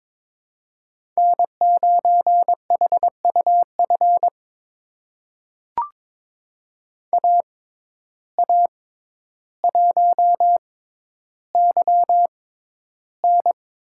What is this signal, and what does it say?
Morse code, heard as a keyed tone, about 22 wpm, spaced as for 11 wpm, 700 Hz. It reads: N9HUF AA1YN
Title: USA Call Signs - 5 Characters - 6x Character Spacing - 22wpm